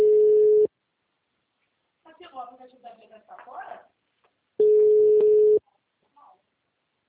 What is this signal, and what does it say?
Ringback tone (Europe, 425 Hz), with speech.
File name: out- -2004- - - .3298